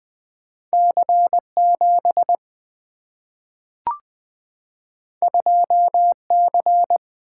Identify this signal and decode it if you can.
Morse code, heard as a keyed tone, about 20 wpm, 700 Hz. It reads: C7 2C
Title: Single-Letter Single-Number 20wpm